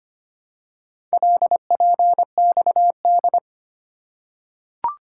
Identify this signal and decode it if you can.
Morse code, heard as a keyed tone, about 25 wpm, 700 Hz. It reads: LPXD
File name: practice / 25wpm / Four Letter-Number Combinations - 25wpm